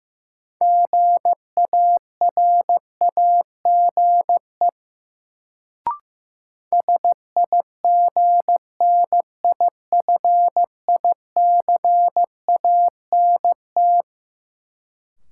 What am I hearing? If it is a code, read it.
Morse code, heard as a keyed tone, about 15 wpm, 700 Hz. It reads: GARAGE SIGNIFICANT
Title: Sentences from Top 1600 Words - New Word Drill Alt - 15wpm